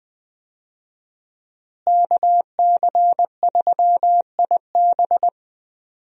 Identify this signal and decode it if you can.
Morse code, heard as a keyed tone, about 20 wpm, 700 Hz. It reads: KC3IB
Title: USA Call Signs - 5 Characters - Repeated 3x - 20wpm